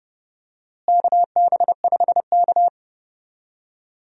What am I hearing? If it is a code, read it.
Morse code, heard as a keyed tone, about 32 wpm, 700 Hz. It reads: KB5K
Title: USA Call Signs - 4 Characters - Repeated 3x - 30wpm